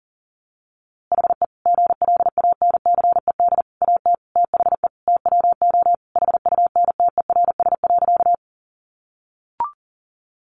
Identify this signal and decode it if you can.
Morse code, heard as a keyed tone, about 40 wpm, 700 Hz. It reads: HE GLANCED AT THE TWO HUNTERS.
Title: A Sound Of Thunder 40wpm